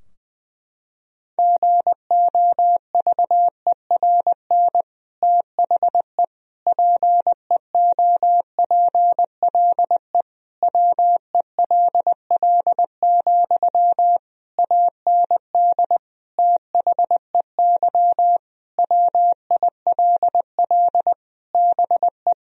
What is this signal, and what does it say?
Morse code, heard as a keyed tone, about 20 wpm, 700 Hz. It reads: GOVERN THE PEOPLE WELL, AND THEY WILL BE
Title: Sentences from Top 1000 Words - Review Part 2 - 20wpm